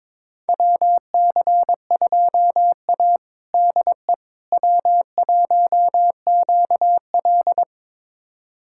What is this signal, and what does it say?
Morse code, heard as a keyed tone, about 21 wpm, 700 Hz. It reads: WC2A DE W1QL
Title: USA Callsign DE Callsign - 4 Characters - 22wpm